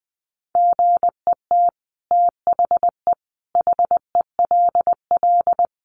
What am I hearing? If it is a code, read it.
Morse code, heard as a keyed tone, about 20 wpm, 700 Hz. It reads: GET THE HELL
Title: Sets of 5 Words Encore2 20wpm